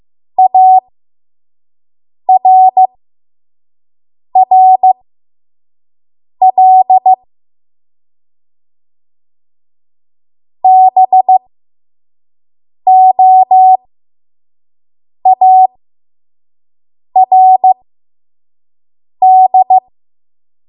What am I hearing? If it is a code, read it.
Morse code, heard as a keyed tone, about 15 wpm, spaced as for 5 wpm, 750 Hz. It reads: ARRL BOARD
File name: Morse / 05WPM